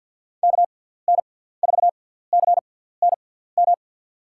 Morse code, heard as a keyed tone, about 53 wpm, spaced as for 17 wpm, 700 Hz. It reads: KN4CNM
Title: USA Call Signs - 6 Characters - 6x Character Spacing - 50wpm